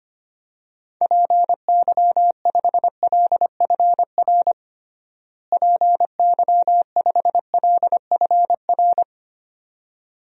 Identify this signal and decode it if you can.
Morse code, heard as a keyed tone, about 25 wpm, 700 Hz. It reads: PY5LFR PY5LFR
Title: DXCC-100 Call Signs - 6 Characters - Repeated 2x - 25wpm